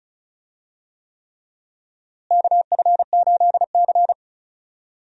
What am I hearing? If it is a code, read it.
Morse code, heard as a keyed tone, about 34 wpm, 700 Hz. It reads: KF8C
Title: USA Call Signs - 4 Characters - Repeated 3x - 35wpm